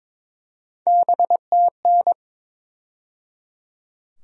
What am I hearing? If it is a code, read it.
Morse code, heard as a keyed tone, about 22 wpm, 700 Hz. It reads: BTN
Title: USA Call Sign Postfixes - 3 Letters 22wpm